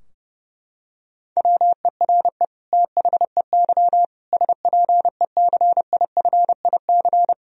Morse code, heard as a keyed tone, about 30 wpm, 700 Hz. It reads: WERE THEY SPECIFIC
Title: Sentences from Top 1000 Words - Review Part 3 - 30wpm